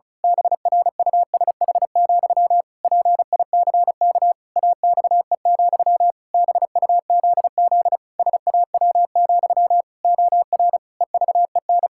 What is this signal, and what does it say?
Morse code, heard as a keyed tone, about 34 wpm, 700 Hz. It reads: BRUSH, PICK AXE, BUZZ SAW, OR EVEN
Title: Pangram Challenge v2 - 35wpm